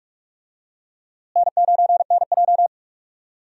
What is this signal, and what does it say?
Morse code, heard as a keyed tone, about 43 wpm, 700 Hz. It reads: N9NJ
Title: USA Call Signs - 4 Characters - Repeated 2x - 45wpm